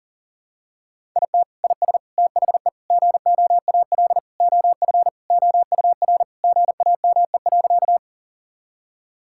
Morse code, heard as a keyed tone, about 40 wpm, 690 Hz. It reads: IT IS THE GOAL OF OUR GAME.
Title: Sentences from Top 1000 Words - Review Part 5 - 40wpm